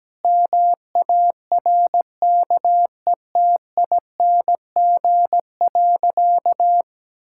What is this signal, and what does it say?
Morse code, heard as a keyed tone, about 17 wpm, 700 Hz. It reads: MARKETING.
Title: Sentences from Top 1000 Words - Review Part 5 - 17wpm